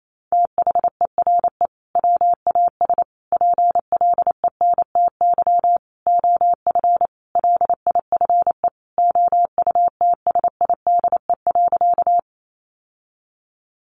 Morse code, heard as a keyed tone, about 28 wpm, 700 Hz. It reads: THERE WAS PLENTY OF LIFE OUTSIDE.
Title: Sentences from Top 1300 Words 28wpm